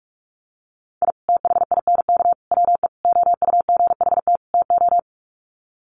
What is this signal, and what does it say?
Morse code, heard as a keyed tone, about 44 wpm, 700 Hz. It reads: I THINK WE OUGHT TO
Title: Sets of 5 Words Encore 45wpm